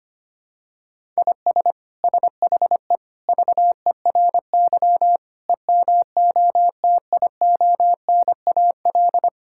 Morse code, heard as a keyed tone, about 25 wpm, 700 Hz. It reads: IS SHE VERY EMOTIONAL
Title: Sentences from Top 1000 Words - Review Part 5 - 25wpm